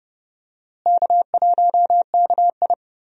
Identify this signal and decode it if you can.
Morse code, heard as a keyed tone, about 31 wpm, 700 Hz. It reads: K1KI
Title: USA Call Signs - 4 Characters - Repeated 2x - 30wpm